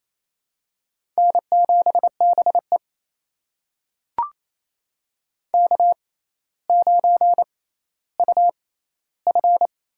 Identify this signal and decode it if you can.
Morse code, heard as a keyed tone, about 28 wpm, spaced as for 14 wpm, 700 Hz. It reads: N7BE K9UF
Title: USA Call Signs - 4 Characters - 6x Character Spacing - 28wpm